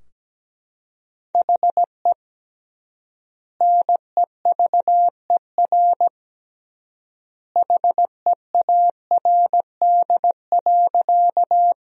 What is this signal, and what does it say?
Morse code, heard as a keyed tone, about 17 wpm, 700 Hz. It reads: HE NEVER HEARD.
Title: 3x Word-Spacing - Sentences from Top 400 Words - 17wpm